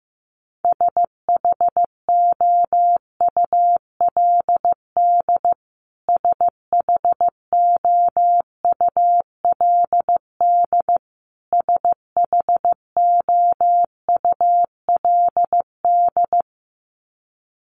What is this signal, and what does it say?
Morse code, heard as a keyed tone, about 15 wpm, 700 Hz. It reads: SHOULD SHOULD SHOULD
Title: Top 300 Words Repeated 15wpm